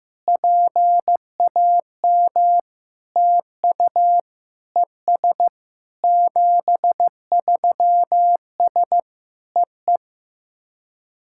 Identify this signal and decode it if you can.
Morse code, heard as a keyed tone, about 14 wpm, 700 Hz. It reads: PAM TU ES 73S EE
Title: SOTA QSO Practice 15wpm